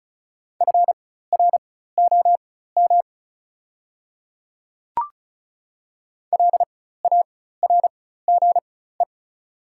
Morse code, heard as a keyed tone, about 36 wpm, spaced as for 16 wpm, 690 Hz. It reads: FROM LARGE
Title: 4x Character Spacing - Top 500 Words - 35wpm